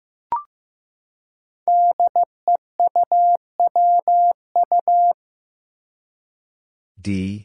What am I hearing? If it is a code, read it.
Morse code, heard as a keyed tone, about 15 wpm, 700 Hz. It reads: DEUWU
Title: Five Challenging Letter-Number Combinations 15wpm